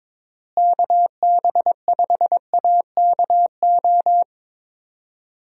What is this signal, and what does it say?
Morse code, heard as a keyed tone, about 22 wpm, 700 Hz. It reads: KB5AKO
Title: USA Call Signs - 6 Characters - Repeated 2x - 22wpm